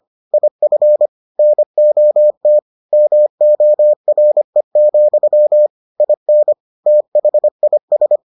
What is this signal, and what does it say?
Morse code, heard as a keyed tone, about 25 wpm, 600 Hz. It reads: IF NOT MORE, IN THIS